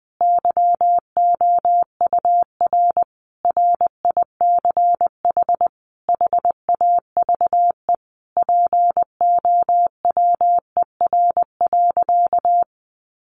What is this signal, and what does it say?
Morse code, heard as a keyed tone, about 20 wpm, 700 Hz. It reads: YOUR RICH HAVE POWER.
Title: Sentences from Top 1100 Words 20wpm